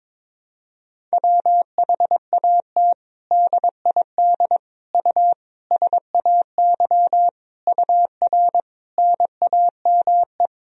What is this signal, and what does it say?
Morse code, heard as a keyed tone, about 22 wpm, 700 Hz. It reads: WHAT DID U SAY UR NAME
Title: Sentences from Top 200 Words and Abbreviations 22wpm